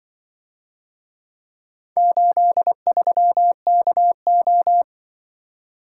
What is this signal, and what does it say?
Morse code, heard as a keyed tone, about 24 wpm, 700 Hz. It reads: 83KO
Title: Speed-Racing - Four Letter-Number Combinations - 20wpm